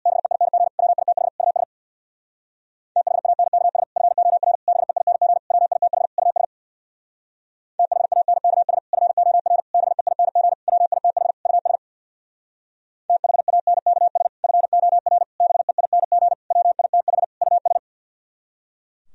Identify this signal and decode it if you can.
Morse code, heard as a keyed tone, about 60 wpm, 700 Hz. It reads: BEING WITH US THANKS FOR BEING WITH US THANKS FOR BEING WITH US THANKS FOR BEING WITH US
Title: Speed-Racing - Sets of 5 Words 50wpm